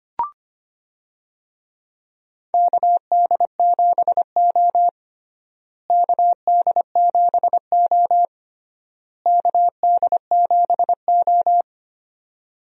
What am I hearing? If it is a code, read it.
Morse code, heard as a keyed tone, about 25 wpm, 700 Hz. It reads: KD7O KD7O KD7O
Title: USA Call Signs - 4 Characters - Repeated 3x - 25wpm